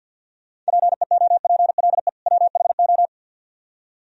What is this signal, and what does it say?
Morse code, heard as a keyed tone, about 62 wpm, 700 Hz. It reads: PEOPLE WHO
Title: Speed-Racing - Sets of 2 Words 50wpm